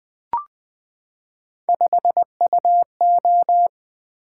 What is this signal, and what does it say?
Morse code, heard as a keyed tone, about 20 wpm, 700 Hz. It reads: 5UO
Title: Three Letter-Number Combinations - 20wpm